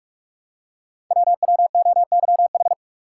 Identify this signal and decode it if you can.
Morse code, heard as a keyed tone, about 44 wpm, 690 Hz. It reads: WWOYH
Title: Code Groups - 5 Letters - 45wpm